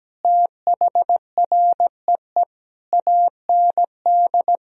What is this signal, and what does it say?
Morse code, heard as a keyed tone, about 17 wpm, 700 Hz. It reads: THREE AND
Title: Sentences from Top 1000 Words - Review Part 3 - 17wpm